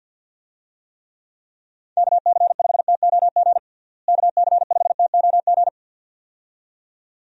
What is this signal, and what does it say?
Morse code, heard as a keyed tone, about 50 wpm, 700 Hz. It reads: KC5TOG KC5TOG
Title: USA Call Signs - 6 Characters - Repeated 2x - 50wpm